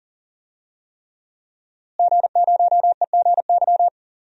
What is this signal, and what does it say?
Morse code, heard as a keyed tone, about 40 wpm, 700 Hz. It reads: G0EGY
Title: DXCC-100 Call Signs - 5 Characters - Repeated 3x - 40wpm